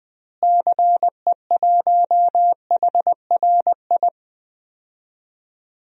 Morse code, heard as a keyed tone, about 20 wpm, 700 Hz. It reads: CE1HRI
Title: DXCC-100 Call Signs - 6 Characters - Repeated 2x - 20wpm